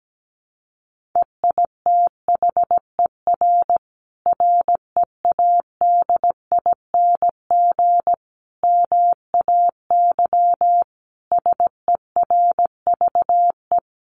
Morse code, heard as a keyed tone, about 17 wpm, 700 Hz. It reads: EITHER READING MAY SERVE
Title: Sentences from Top 1400 Words 17wpm